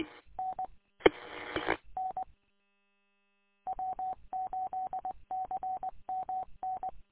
Morse code, heard as a keyed tone, about 20 wpm, 750 Hz. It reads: N E N W8CMN
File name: w8cmn- -combined